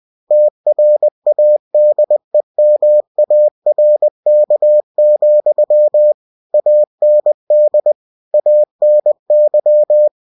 Morse code, heard as a keyed tone, about 20 wpm, 600 Hz. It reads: TRADEMARK, AND ANY